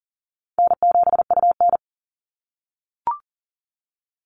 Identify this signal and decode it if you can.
Morse code, heard as a keyed tone, about 37 wpm, 700 Hz. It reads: N7UN
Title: NA SOTA Chaser Callsigns 40wpm